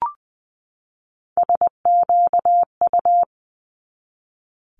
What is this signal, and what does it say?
Morse code, heard as a keyed tone, about 20 wpm, 700 Hz. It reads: SQU
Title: Multi-letter Phonemes 20wpm